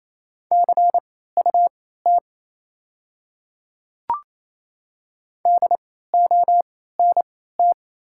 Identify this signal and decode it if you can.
Morse code, heard as a keyed tone, about 27 wpm, spaced as for 16 wpm, 700 Hz. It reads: CUT DONT
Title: 3x Character Spacing - Top 300 Words - 28wpm